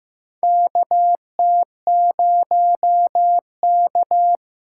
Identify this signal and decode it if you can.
Morse code, heard as a keyed tone, about 15 wpm, 700 Hz. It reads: KT0K
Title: K1USN SST QSO Practice - 15wpm